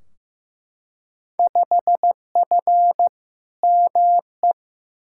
Morse code, heard as a keyed tone, about 15 wpm, 700 Hz. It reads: 5F ME
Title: Field Day Exchanges Encore - 15wpm